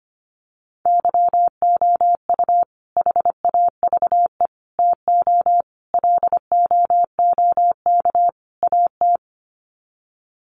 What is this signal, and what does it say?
Morse code, heard as a keyed tone, about 25 wpm, 700 Hz. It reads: YOU HAVE TO LOOK AT
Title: Sets of 5 Words Encore 25wpm